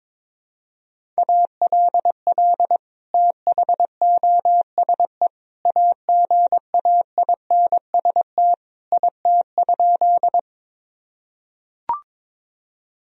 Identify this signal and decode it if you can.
Morse code, heard as a keyed tone, about 21 wpm, 700 Hz. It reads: ALL THOSE AGAINST IT?
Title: Sentences from Top 1000 Words - Review Part 1 - 22wpm